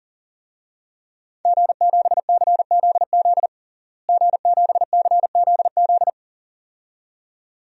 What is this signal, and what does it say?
Morse code, heard as a keyed tone, about 40 wpm, 700 Hz. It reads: G7CZZ G7CZZ
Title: DXCC-100 Call Signs - 5 Characters - Repeated 2x - 40wpm